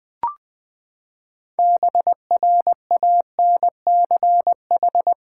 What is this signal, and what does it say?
Morse code, heard as a keyed tone, about 21 wpm, 700 Hz. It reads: BRANCH
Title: Practice QSO - Exchange 3 - More Info - 20wpm